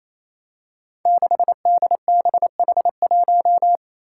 Morse code, heard as a keyed tone, about 27 wpm, 700 Hz. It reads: 6DBH1
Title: Five Challenging Letter-Number Combinations 28wpm